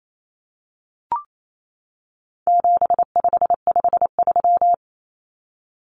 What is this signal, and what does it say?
Morse code, heard as a keyed tone, about 28 wpm, 700 Hz. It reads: 7553
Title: Numbers - 4-Digit 28wpm